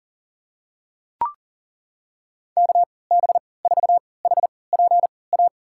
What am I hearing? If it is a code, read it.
Morse code, heard as a keyed tone, about 40 wpm, spaced as for 23 wpm, 700 Hz. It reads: KB4HPA
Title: USA Call Signs - 6 Characters - 3x Character Spacing - 40wpm